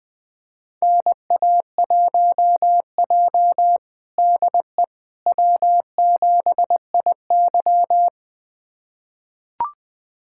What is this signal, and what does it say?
Morse code, heard as a keyed tone, about 21 wpm, 700 Hz. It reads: NA1J DE W7IY
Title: USA Callsign DE Callsign - 4 Characters - 20wpm